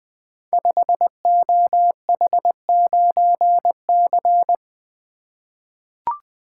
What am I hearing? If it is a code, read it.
Morse code, heard as a keyed tone, about 20 wpm, 700 Hz. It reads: 5OH9C
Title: Code Groups - 5 Letter:Numbers - 20wpm